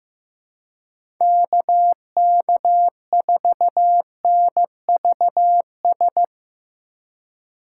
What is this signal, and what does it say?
Morse code, heard as a keyed tone, about 17 wpm, 700 Hz. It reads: KK4NVS
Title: Speed Racing - USA Call Signs 15wpm